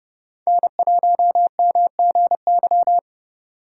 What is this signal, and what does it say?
Morse code, heard as a keyed tone, about 30 wpm, 700 Hz. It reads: N1MGY
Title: USA Call Signs - 4x Character Spacing - 30wpm